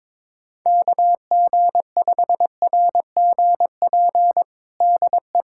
Morse code, heard as a keyed tone, about 22 wpm, 700 Hz. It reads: KG5RGP DE
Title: USA Callsign DE Callsign 22wpm